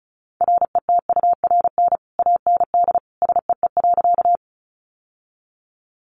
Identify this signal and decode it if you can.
Morse code, heard as a keyed tone, about 34 wpm, 700 Hz. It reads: RETURN AND SEE.
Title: Sentences from Top 800 Words 35wpm